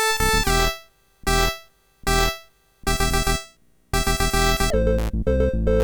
There is no code.